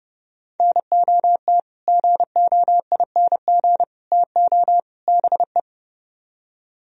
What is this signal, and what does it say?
Morse code, heard as a keyed tone, about 40 wpm, spaced as for 35 wpm, 700 Hz. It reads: NOT GOING TO BE
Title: Speed-Racing - Sets of 4 Words 30wpm